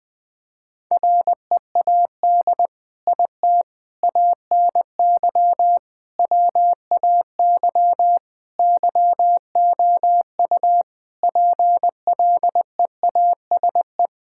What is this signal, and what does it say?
Morse code, heard as a keyed tone, about 20 wpm, 700 Hz. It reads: READ IT ANY WAY YOU PLEASE
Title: Sentences from Top 1600 Words 20wpm